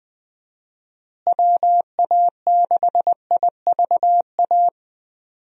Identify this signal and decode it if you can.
Morse code, heard as a keyed tone, about 20 wpm, 700 Hz. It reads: WA6IVA
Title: USA Call Signs - 6 Characters - Repeated 3x - 20wpm